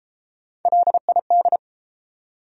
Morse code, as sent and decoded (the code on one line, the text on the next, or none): .-.. .. -..
LID